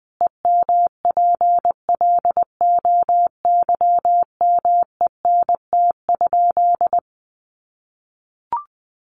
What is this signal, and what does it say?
Morse code, heard as a keyed tone, about 19 wpm, 700 Hz. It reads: EMPLOYMENT?
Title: Sentences from Top 1500 Words 20wpm